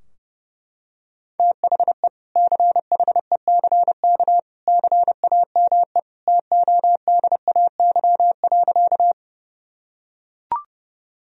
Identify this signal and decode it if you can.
Morse code, heard as a keyed tone, about 30 wpm, 700 Hz. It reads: THE CHECK CAME TODAY.
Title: Sentences from Top 1000 Words - Review Part 3 - 30wpm